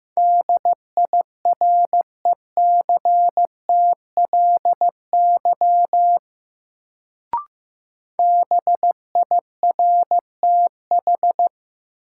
Morse code, heard as a keyed tone, about 15 wpm, 700 Hz. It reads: DIRECTLY BIRTH
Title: Sentences from Top 800 Words - New Word Drill - 15wpm